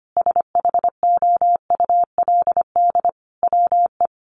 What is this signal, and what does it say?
Morse code, heard as a keyed tone, about 25 wpm, 700 Hz. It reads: SHOULD WE
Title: Sentences from Top 1100 Words 25wpm